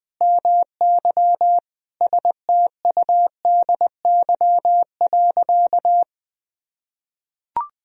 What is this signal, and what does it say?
Morse code, heard as a keyed tone, about 20 wpm, 700 Hz. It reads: MY STUDY.
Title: Sentences from Top 1000 Words - Review Part 1 - 20wpm